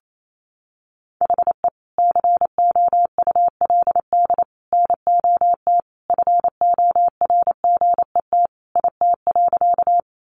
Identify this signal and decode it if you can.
Morse code, heard as a keyed tone, about 28 wpm, 700 Hz. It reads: HE COULD NOT FORGET IT.
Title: Sentences from Top 1200 Words 28wpm